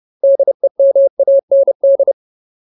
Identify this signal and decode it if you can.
Morse code, heard as a keyed tone, about 30 wpm, 550 Hz. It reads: DEMAND
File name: Ultimate Word List / 30WPM Ultimate